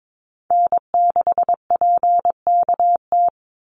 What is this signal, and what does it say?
Morse code, heard as a keyed tone, about 22 wpm, 700 Hz. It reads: N6PKT
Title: NA SOTA Chaser Callsigns 22wpm